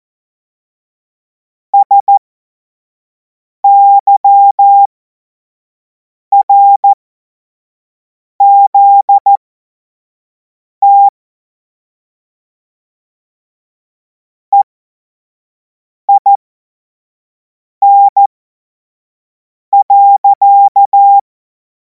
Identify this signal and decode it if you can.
Morse code, heard as a keyed tone, about 14 wpm, spaced as for 5 wpm, 800 Hz. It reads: SYRZT EIN.
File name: Morsekurs / Witze / Witz 19. Mai 2026 25 F70